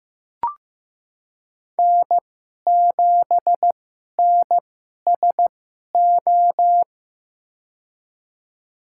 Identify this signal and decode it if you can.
Morse code, heard as a keyed tone, about 15 wpm, spaced as for 11 wpm, 700 Hz. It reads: N7NSO
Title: USA Call Signs - 5 Characters - 2x Character Spacing - 15wpm